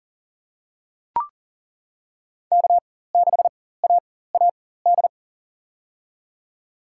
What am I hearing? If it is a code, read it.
Morse code, heard as a keyed tone, about 39 wpm, spaced as for 18 wpm, 700 Hz. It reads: K6AAD
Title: USA Call Signs - 5 Characters - 4x Character Spacing - 40wpm